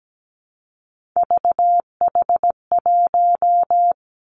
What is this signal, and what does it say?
Morse code, heard as a keyed tone, about 17 wpm, 700 Hz. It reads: VH1
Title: Three Challenging Letter-Number Combinations 17wpm